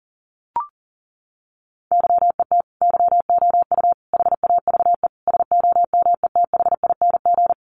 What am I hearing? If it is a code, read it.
Morse code, heard as a keyed tone, about 40 wpm, 700 Hz. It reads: YET YOU HAVE SOMETHING
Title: Sentences from Top 600 Words 40wpm